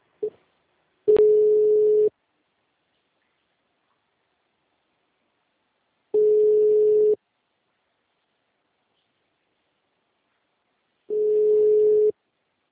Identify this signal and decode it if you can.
Ringback tone (Europe, 425 Hz).
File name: out- -2002- - - .31355